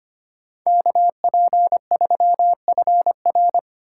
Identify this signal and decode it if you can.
Morse code, heard as a keyed tone, about 33 wpm, 700 Hz. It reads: KP3FR